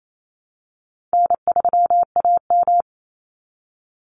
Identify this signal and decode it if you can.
Morse code, heard as a keyed tone, about 28 wpm, 700 Hz. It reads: N3AM
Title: Call Signs - 28wpm